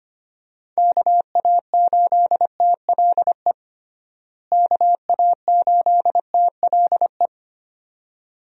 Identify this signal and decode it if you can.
Morse code, heard as a keyed tone, about 25 wpm, 700 Hz. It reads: KA8TLE KA8TLE
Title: USA Call Signs - 6 Characters - Repeated 2x - 25wpm